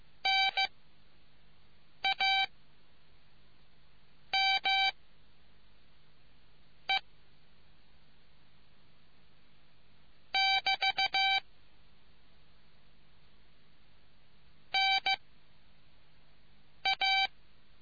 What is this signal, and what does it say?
Morse code, heard as a keyed tone, about 15 wpm, 780 Hz. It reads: N A M E <BT> N A